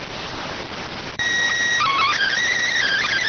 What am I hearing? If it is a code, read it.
SSTV (slow-scan television), PD 290.